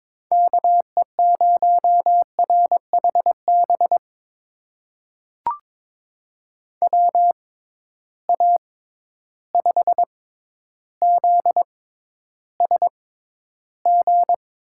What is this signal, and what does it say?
Morse code, heard as a keyed tone, about 22 wpm, spaced as for 11 wpm, 700 Hz. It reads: KE0RHB WA5ZSG
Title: USA Call Signs - 6 Characters - 6x Character Spacing - 22wpm